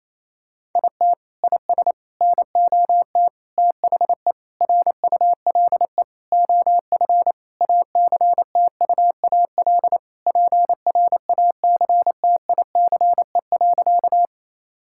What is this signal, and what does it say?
Morse code, heard as a keyed tone, about 28 wpm, 700 Hz. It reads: IT IS NOT THE RULE OF ACTUAL PRACTICE.
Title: Sentences from Top 1000 Words - Review Part 5 - 28wpm